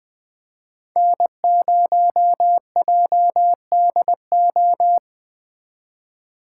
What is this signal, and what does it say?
Morse code, heard as a keyed tone, about 20 wpm, 700 Hz. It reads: N0JDO
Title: Speed Racing - USA 1x3 Call Signs 17wpm